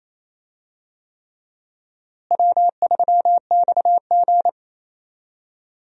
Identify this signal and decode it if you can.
Morse code, heard as a keyed tone, about 28 wpm, 700 Hz. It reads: W3XG
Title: USA Call Signs - 4 Characters - Repeated 3x - 28wpm